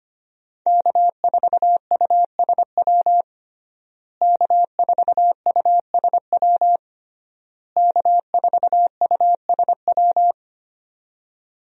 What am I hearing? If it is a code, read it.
Morse code, heard as a keyed tone, about 25 wpm, 700 Hz. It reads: K4USW K4USW K4USW
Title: USA Call Signs - 5 Characters - Repeated 3x - 25wpm